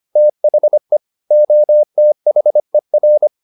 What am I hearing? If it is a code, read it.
Morse code, heard as a keyed tone, about 25 wpm, 600 Hz. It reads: THE OTHER
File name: the old man and the sea part 1 25wpm